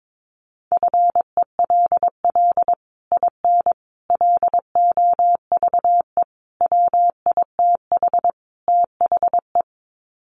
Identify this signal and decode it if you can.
Morse code, heard as a keyed tone, about 22 wpm, 700 Hz. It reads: FELL IN LOVE WITH THE
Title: Sets of 5 Words Encore2 22wpm